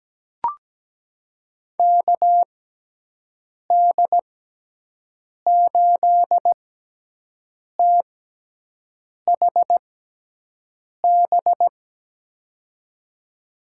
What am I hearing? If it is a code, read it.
Morse code, heard as a keyed tone, about 17 wpm, spaced as for 6 wpm, 700 Hz. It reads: KD8THB
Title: USA Call Signs - 6x Character Spacing - 17wpm